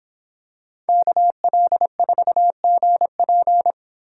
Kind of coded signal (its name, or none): Morse code